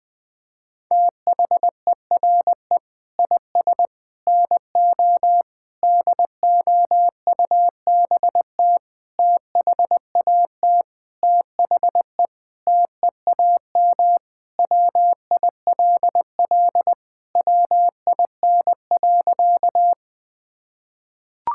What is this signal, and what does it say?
Morse code, heard as a keyed tone, about 20 wpm, 700 Hz. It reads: THERE IS NO DOUBT THAT THE TEAM WILL WIN.
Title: Sentences with 6-Word Phrases - 20wpm